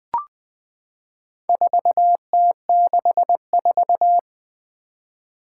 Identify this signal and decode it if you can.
Morse code, heard as a keyed tone, about 20 wpm, 700 Hz. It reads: 4T64
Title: Four Challenging Letter-Number Combinations 20wpm